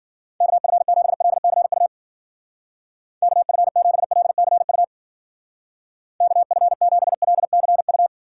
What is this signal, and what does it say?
Morse code, heard as a keyed tone, about 50 wpm, 700 Hz. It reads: KF7LCU KF7LCU KF7LCU
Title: Speed Racing - USA Call Signs 40wpm